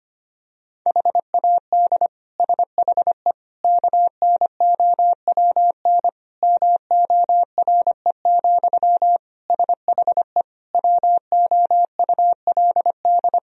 Morse code, heard as a keyed tone, about 25 wpm, 700 Hz. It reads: HAD SHE KNOWN MORE, SHE WOULD
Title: Sentences from Top 1000 Words - Review Part 3 - 25wpm